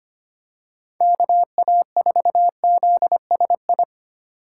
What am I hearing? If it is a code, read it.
Morse code, heard as a keyed tone, about 25 wpm, 700 Hz. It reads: KA4ZSI
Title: USA Call Signs - 6 Characters - 3x Character Spacing - 25wpm